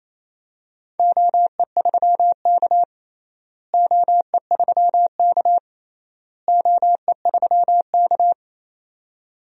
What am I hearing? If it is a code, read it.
Morse code, heard as a keyed tone, about 28 wpm, 700 Hz. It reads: OE3K OE3K OE3K
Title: DXCC-100 Call Signs - 4 Characters - Repeated 3x - 28wpm